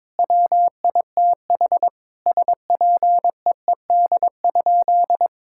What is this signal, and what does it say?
Morse code, heard as a keyed tone, about 22 wpm, 700 Hz. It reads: WITH SPEED?
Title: Sentences from Top 1000 Words - Review Part 5 - 22wpm